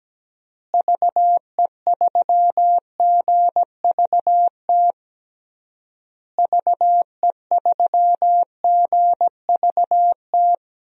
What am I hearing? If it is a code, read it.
Morse code, heard as a keyed tone, about 17 wpm, 700 Hz. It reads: VE3GVT VE3GVT
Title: DXCC-100 Call Signs - 6 Characters - Repeated 3x - 17wpm